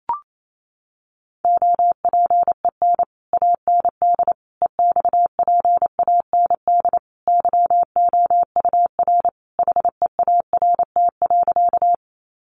Morse code, heard as a keyed tone, about 28 wpm, 700 Hz. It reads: OPEN AND EXPAND YOUR HEART.
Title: Sentences from Top 1400 Words 28wpm